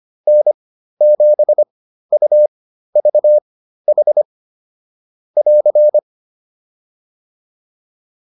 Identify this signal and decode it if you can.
Morse code, heard as a keyed tone, about 25 wpm, spaced as for 13 wpm, 600 Hz. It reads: N7UVH <AR>